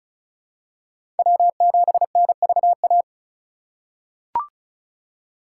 Morse code, heard as a keyed tone, about 34 wpm, 700 Hz. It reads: W7NVA
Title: USA Call Signs - 5 Characters - 4x Character Spacing - 35wpm